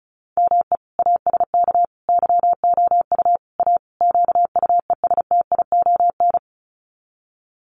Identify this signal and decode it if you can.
Morse code, heard as a keyed tone, about 34 wpm, 700 Hz. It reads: ME ASK YOU A QUESTION
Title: Sets of 5 Words Encore2 35wpm